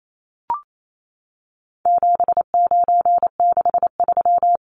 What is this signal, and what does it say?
Morse code, heard as a keyed tone, about 28 wpm, 700 Hz. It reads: 7963
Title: Numbers - 4-Digit 28wpm